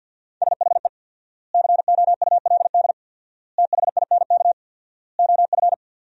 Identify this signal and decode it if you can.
Morse code, heard as a keyed tone, about 50 wpm, 700 Hz. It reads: SHE COULD THINK OF
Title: 4x Word-Spacing - Sentences from Top 500 Words - 50wpm